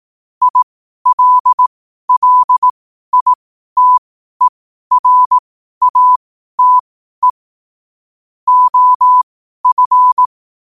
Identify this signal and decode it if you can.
Morse code, heard as a keyed tone, about 18 wpm, spaced as for 12 wpm, 1000 Hz. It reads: ILLITERATE OF
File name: qotdfiles / 12wpm